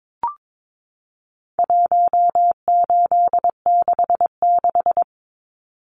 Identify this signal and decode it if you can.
Morse code, heard as a keyed tone, about 22 wpm, 700 Hz. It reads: 1866
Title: Numbers - 4-Digit 22wpm